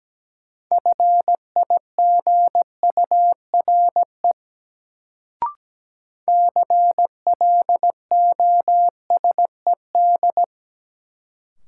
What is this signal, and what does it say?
Morse code, heard as a keyed tone, about 17 wpm, 700 Hz. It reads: FIGURE CLOSED
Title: Sentences from Top 1400 Words - New Word Drill Alt - 17wpm